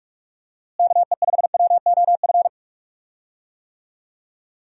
Morse code, heard as a keyed tone, about 46 wpm, 700 Hz. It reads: KE5WOF
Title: USA Call Signs - 6 Characters - Repeated 3x - 45wpm